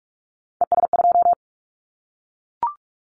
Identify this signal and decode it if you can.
Morse code, heard as a keyed tone, about 44 wpm, 700 Hz. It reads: ES2
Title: Callsign Prefixes for Top 100 DXCC - 45wpm